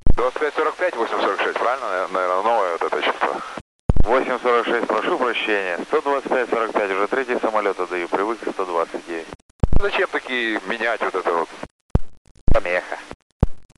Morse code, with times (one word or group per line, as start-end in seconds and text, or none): none